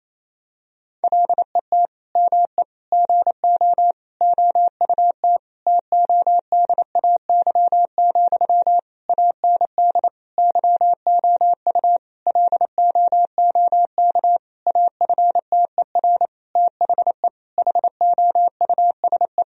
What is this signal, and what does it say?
Morse code, heard as a keyed tone, about 27 wpm, 700 Hz. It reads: LET ME GO OUT TODAY, AND YOU LOOK AFTER THE HOUSE
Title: Sentences from Top 1000 Words - Review Part 4 - 28wpm